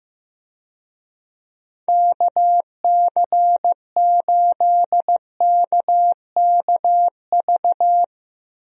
Morse code, heard as a keyed tone, about 15 wpm, 700 Hz. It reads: KC8KKV
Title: USA Call Signs - 6 Characters - Repeated 2x - 15wpm